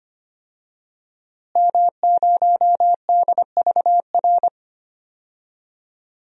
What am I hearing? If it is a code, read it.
Morse code, heard as a keyed tone, about 25 wpm, 700 Hz. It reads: M0DVR
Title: DXCC-100 Call Signs - 5 Characters - Repeated 2x - 25wpm